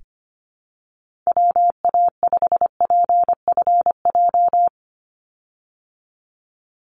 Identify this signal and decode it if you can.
Morse code, heard as a keyed tone, about 25 wpm, 700 Hz. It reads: WA5PFJ
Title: Call Signs Repeated 2x - 25wpm